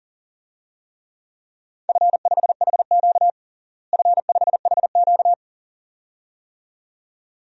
Morse code, heard as a keyed tone, about 37 wpm, 700 Hz. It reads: F5HQ F5HQ
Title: DXCC-100 Call Signs - 4 Characters - Repeated 2x - 40wpm